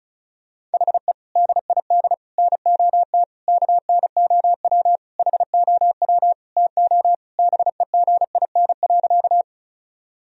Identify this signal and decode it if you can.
Morse code, heard as a keyed tone, about 36 wpm, 700 Hz. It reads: HE DID NOT KNOW HOW TO BEGIN.
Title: Sentences from Top 1000 Words - Review Part 1 - 35wpm